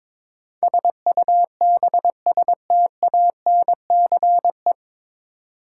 Morse code, heard as a keyed tone, about 22 wpm, 700 Hz. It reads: SUBSTANCE
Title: Sentences from Top 1500 Words - New Word Drill Alt - 22wpm